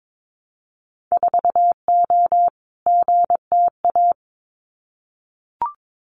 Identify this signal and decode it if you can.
Morse code, heard as a keyed tone, about 22 wpm, 700 Hz. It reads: 4O GTA
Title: Winter Field Day Exchanges 22wpm